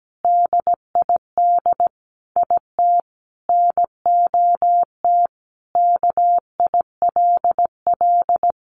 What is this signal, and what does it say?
Morse code, heard as a keyed tone, about 17 wpm, 700 Hz. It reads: DID IT NOT KILL
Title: Sentences from Top 1400 Words 17wpm